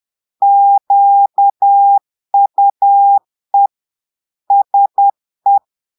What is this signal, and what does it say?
Morse code, heard as a keyed tone, about 10 wpm, 800 Hz. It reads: QUE SE
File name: Portugais - texte en clair 1 - vitesse - 10 mots minute